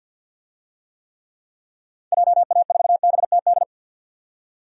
Morse code, heard as a keyed tone, about 50 wpm, 700 Hz. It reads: JA4BTD
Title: DXCC-100 Call Signs - 6 Characters - 50wpm